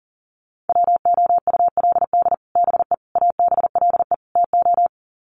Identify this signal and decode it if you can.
Morse code, heard as a keyed tone, about 40 wpm, 700 Hz. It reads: WOULD BE ABLE TO
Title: Sets of 4 Words 40wpm